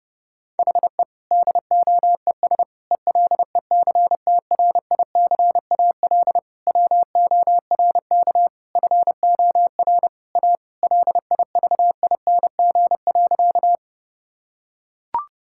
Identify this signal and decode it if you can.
Morse code, heard as a keyed tone, about 30 wpm, 700 Hz. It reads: HE DOES ELECTRICAL WORK FOR A LIVING.
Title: Sentences from Top 1000 Words - Review Part 6 - 30wpm